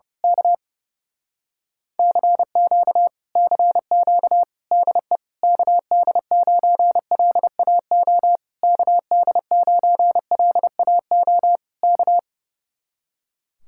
Morse code, heard as a keyed tone, about 29 wpm, 700 Hz. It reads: K CQ CQ DE KD9LAO KD9LAO K
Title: Speed-Racing Common QSO Phrases Encore - 30wpm